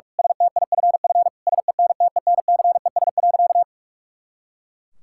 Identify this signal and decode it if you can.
Morse code, heard as a keyed tone, about 48 wpm, 700 Hz. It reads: STIFF SENTENCES.
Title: Pangram Challenge v2 - 45wpm